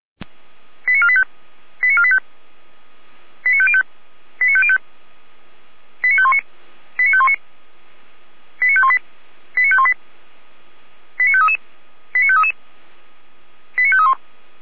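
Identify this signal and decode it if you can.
Five-tone selective call (ZVEI), heard as quick runs of five tones, tones of about 70 ms. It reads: 89485 89485 89505 89505 89519 89519 89518 89518 89533 89533 89531
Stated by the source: ZVEI five-tone selective call received via SDR.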